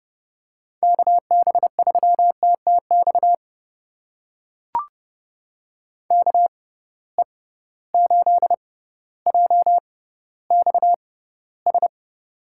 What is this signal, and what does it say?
Morse code, heard as a keyed tone, about 30 wpm, spaced as for 15 wpm, 700 Hz. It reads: KB3TTX KE8JXS
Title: USA Call Signs - 6 Characters - 6x Character Spacing - 30wpm